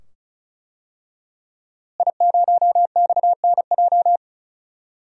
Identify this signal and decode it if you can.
Morse code, heard as a keyed tone, about 36 wpm, 700 Hz. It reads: I0XNJ
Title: DXCC-100 Call Signs - 5 Characters - Repeated 2x - 35wpm